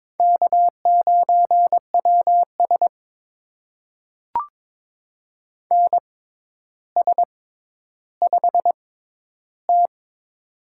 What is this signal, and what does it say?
Morse code, heard as a keyed tone, about 22 wpm, spaced as for 11 wpm, 700 Hz. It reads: K9WS NS5T
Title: USA Call Signs - 4 Characters - 6x Character Spacing - 22wpm